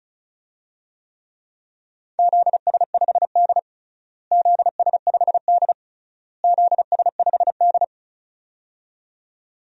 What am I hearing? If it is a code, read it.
Morse code, heard as a keyed tone, about 36 wpm, 700 Hz. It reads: ZS5D ZS5D ZS5D
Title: DXCC-100 Call Signs - 4 Characters - Repeated 3x - 35wpm